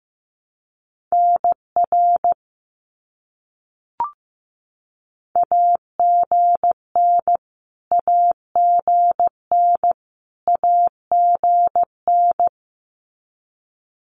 Morse code, heard as a keyed tone, about 15 wpm, 700 Hz. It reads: NR AGN AGN AGN
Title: Abbreviations Repeated 3x - 15wpm